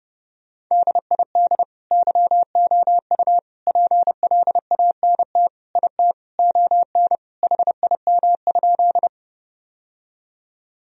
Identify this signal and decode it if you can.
Morse code, heard as a keyed tone, about 30 wpm, 700 Hz. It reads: DID YOU PLANT IT ON HIM?
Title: Sentences from Top 1000 Words - Review Part 1 - 30wpm